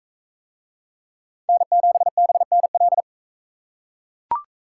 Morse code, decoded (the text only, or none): N7BNL